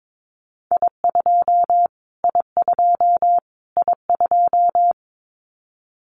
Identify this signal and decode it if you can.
Morse code, heard as a keyed tone, about 22 wpm, 700 Hz. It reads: I2 I2 I2
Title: Callsign Prefixes for Top 25 DXCC - Repeated 3x - 22wpm